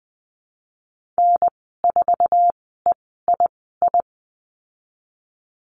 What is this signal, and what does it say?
Morse code, heard as a keyed tone, about 20 wpm, spaced as for 15 wpm, 700 Hz. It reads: N4EII
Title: 055 Lesson 10g - Letter r - Call Signs 20wpm